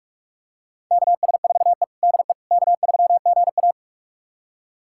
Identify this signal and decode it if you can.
Morse code, heard as a keyed tone, about 45 wpm, 700 Hz. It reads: KS4E DE K3GA
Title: USA Callsign DE Callsign - 4 Characters - 45wpm